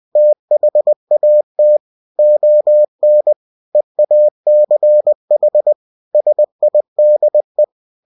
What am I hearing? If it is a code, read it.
Morse code, heard as a keyed tone, about 20 wpm, 600 Hz. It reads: THAT ON EACH SIDE
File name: cw-books / winnie-the-pooh / winnie-20wpm-page-0083